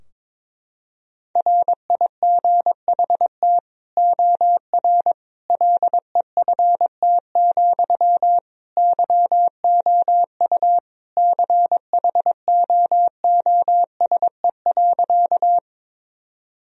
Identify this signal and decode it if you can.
Morse code, heard as a keyed tone, about 22 wpm, 700 Hz. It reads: RIGHT OR LEFT, YOU CHOOSE.
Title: Sentences from Top 1000 Words - Review Part 4 - 22wpm